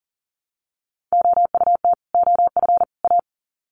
Morse code, heard as a keyed tone, about 40 wpm, 700 Hz. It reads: OUT OF A
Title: Sets of 3 Words 40wpm